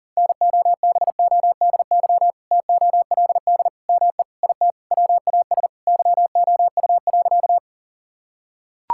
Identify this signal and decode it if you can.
Morse code, heard as a keyed tone, about 40 wpm, 700 Hz. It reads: NOBODY TOLD ME IT WAS YOU.
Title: Sentences from Top 1600 Words 40wpm